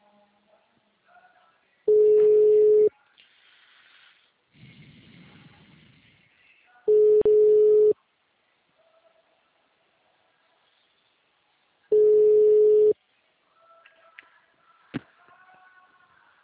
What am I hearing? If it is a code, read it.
Ringback tone (Europe, 425 Hz).
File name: out- -2037- - - .2437